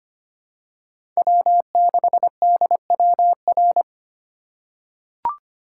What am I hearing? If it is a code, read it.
Morse code, heard as a keyed tone, about 25 wpm, 700 Hz. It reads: W6DWR
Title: USA Call Signs - 5 Characters - 3x Character Spacing - 25wpm